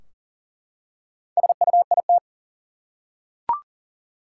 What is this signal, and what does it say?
Morse code, heard as a keyed tone, about 40 wpm, 700 Hz. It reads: SUIT
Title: Four Letter Words - 40wpm